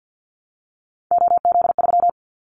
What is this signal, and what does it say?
Morse code, heard as a keyed tone, about 50 wpm, 700 Hz. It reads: OZ3
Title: Callsign Prefixes for Top 100 DXCC - 50wpm